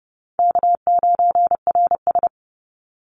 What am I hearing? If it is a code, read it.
Morse code, heard as a keyed tone, about 30 wpm, 700 Hz. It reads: K9RS
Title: Call Signs - 30wpm